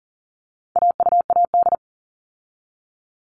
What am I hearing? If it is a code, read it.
Morse code, heard as a keyed tone, about 40 wpm, 700 Hz. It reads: AUAD
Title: All Cut Numbers - 4-Digit 40wpm